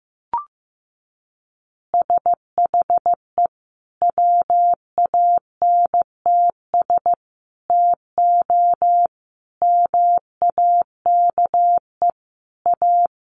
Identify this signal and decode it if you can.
Morse code, heard as a keyed tone, about 15 wpm, 700 Hz. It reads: SHE WANTS TO MAKE A
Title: Easy Phrases 15wpm